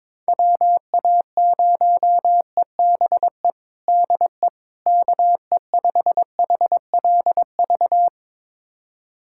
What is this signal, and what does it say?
Morse code, heard as a keyed tone, about 22 wpm, 700 Hz. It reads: WA0EBE DE KE5HLV
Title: USA Callsign DE Callsign - 6 Characters - 22wpm